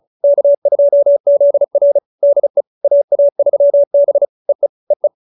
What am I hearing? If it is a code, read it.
Morse code, heard as a keyed tone, about 34 wpm, 570 Hz. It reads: K2ZR DE AA3B EE EE
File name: QSO302 35